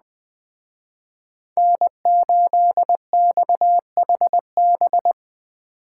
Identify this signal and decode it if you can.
Morse code, heard as a keyed tone, about 20 wpm, 700 Hz. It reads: N8XHB
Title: Speed Racing - USA 1x3 Call Signs 17wpm